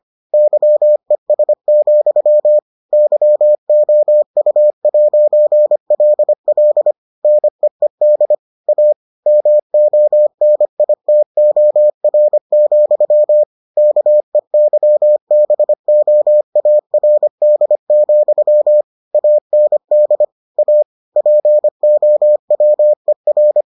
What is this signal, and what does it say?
Morse code, heard as a keyed tone, about 25 wpm, 600 Hz. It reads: YES, YOU'LL NEED A MONITOR, KEYBOARD, AND A POWER